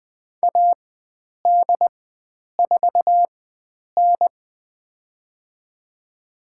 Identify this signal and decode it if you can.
Morse code, heard as a keyed tone, about 20 wpm, spaced as for 9 wpm, 700 Hz. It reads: AD4N
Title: USA Call Signs - 4 Characters - 4x Character Spacing - 20wpm